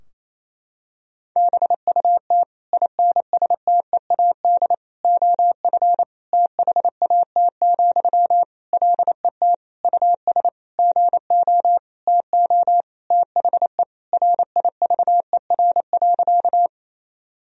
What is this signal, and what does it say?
Morse code, heard as a keyed tone, about 28 wpm, 700 Hz. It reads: BUT INSTEAD OF THAT, LET US GO TO THE RIVER.
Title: Sentences from Top 1000 Words - Review Part 4 - 28wpm